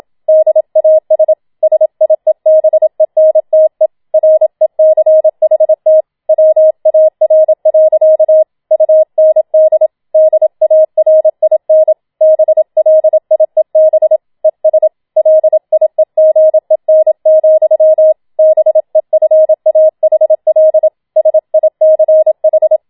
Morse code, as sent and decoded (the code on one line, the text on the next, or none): -.. .- ... / ... .. -... -. - . / .-. -.-. .... - / .-- .- .-. .-.-.- / ..- -. -.. / -.. .- .-. .. -. / -... .-.. .. -... / . ... / .-.. .. --. -. --..-- / -... ..-. .- .... .-.. / ... .. -.-. ....
DAS SIEBENTE RECHT WAR. UND DARIN BLIEB ES LIEGEN, BEFAHL SICH